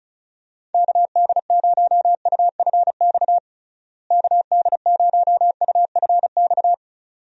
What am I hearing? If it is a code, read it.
Morse code, heard as a keyed tone, about 34 wpm, 700 Hz. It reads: KD0UFX KD0UFX
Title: USA Call Signs - 6 Characters - Repeated 2x - 35wpm